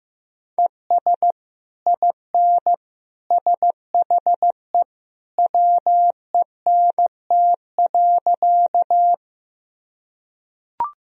Morse code, heard as a keyed tone, about 15 wpm, 700 Hz. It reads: ES IN SHE WENT.
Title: Sentences from Top 200 Words and Abbreviations 15wpm